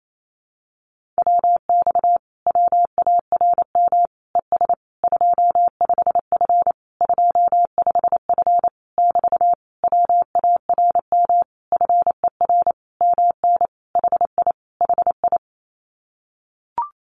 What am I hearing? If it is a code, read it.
Morse code, heard as a keyed tone, about 28 wpm, 700 Hz. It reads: WX WARM ES 25F 25F <BT> WARM FER MN HI HI
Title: Common QSO Phrases Encore 28wpm